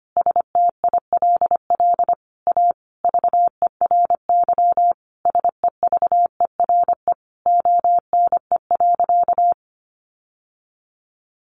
Morse code, heard as a keyed tone, about 25 wpm, 700 Hz. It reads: STILL A VERY SEVERE ONE.
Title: Sentences from Top 1400 Words 25wpm